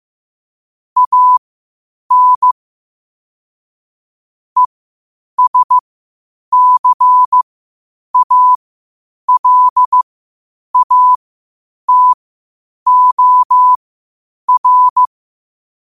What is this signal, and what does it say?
Morse code, heard as a keyed tone, about 15 wpm, spaced as for 8 wpm, 1000 Hz. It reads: AN ESCALATOR
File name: qotdfiles / 8wpm